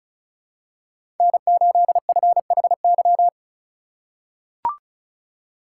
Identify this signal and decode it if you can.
Morse code, heard as a keyed tone, about 36 wpm, 700 Hz. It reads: N8FHY
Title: USA Call Signs - 5 Characters - 2x Character Spacing - 35wpm